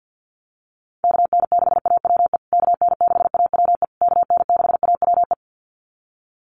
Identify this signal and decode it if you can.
Morse code, heard as a keyed tone, about 50 wpm, 700 Hz. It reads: KN6AWE KN6AWE KN6AWE
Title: USA Call Signs Repeated 3x 50wpm